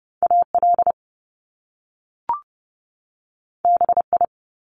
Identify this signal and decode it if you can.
Morse code, heard as a keyed tone, about 30 wpm, 700 Hz. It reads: AL BI
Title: Top Two Letter Combinations in Words 30wpm